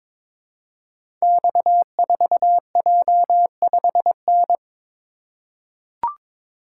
Morse code, heard as a keyed tone, about 22 wpm, 700 Hz. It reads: X4J5N
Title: Code Groups - 5 Letter:Numbers - 22wpm